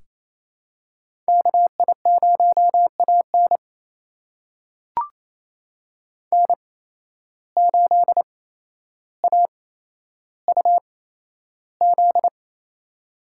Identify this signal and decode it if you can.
Morse code, heard as a keyed tone, about 28 wpm, spaced as for 12 wpm, 700 Hz. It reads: KI0AN N8AUZ
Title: USA Call Signs - 8x Character Spacing - 28wpm